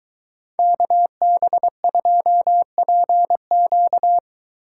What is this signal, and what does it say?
Morse code, heard as a keyed tone, about 22 wpm, 700 Hz. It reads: KB2PQ